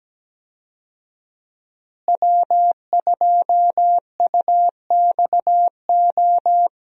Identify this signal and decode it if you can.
Morse code, heard as a keyed tone, about 17 wpm, 700 Hz. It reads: W2UXO
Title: USA Call Signs - 5 Characters - Repeated 2x - 17wpm